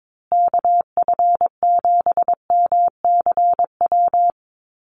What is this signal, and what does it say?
Morse code, heard as a keyed tone, about 22 wpm, 700 Hz. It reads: KF7MCW
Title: USA Call Signs Repeated 2x 22wpm